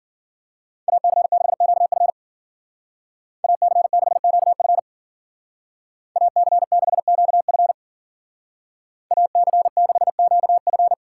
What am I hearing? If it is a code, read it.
Morse code, heard as a keyed tone, about 52 wpm, 700 Hz. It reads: AC6QF AC6QF AC6QF AC6QF